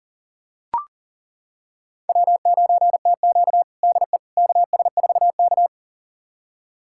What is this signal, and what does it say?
Morse code, heard as a keyed tone, about 40 wpm, 700 Hz. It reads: W9TQ DE KS4K
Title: USA Callsign DE Callsign - 4 Characters - 40wpm